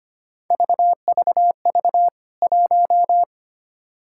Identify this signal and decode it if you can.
Morse code, heard as a keyed tone, about 25 wpm, 700 Hz. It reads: VVV 1
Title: K1USN SST QSO Practice 25wpm